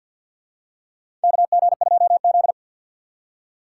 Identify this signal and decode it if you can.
Morse code, heard as a keyed tone, about 52 wpm, 700 Hz. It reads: KG2Z
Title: USA Call Signs - 2x Character Spacing - 50wpm